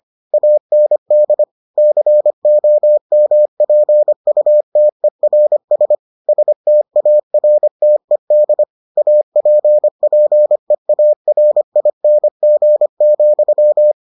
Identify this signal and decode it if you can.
Morse code, heard as a keyed tone, about 25 wpm, 600 Hz. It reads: AND COMPUTERS STARTED APPEARING,